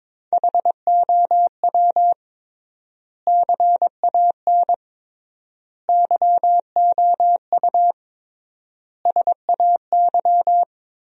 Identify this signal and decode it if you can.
Morse code, heard as a keyed tone, about 22 wpm, 700 Hz. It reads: HOW CAN YOU SAY
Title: 3x Word-Spacing - Sentences from Top 200 Words - 22wpm